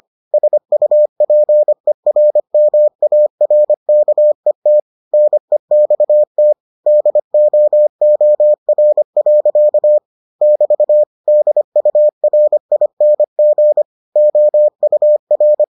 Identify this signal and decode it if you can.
Morse code, heard as a keyed tone, about 25 wpm, 600 Hz. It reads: SUPERMARKET NEXT DOOR. <BT> DURING OUR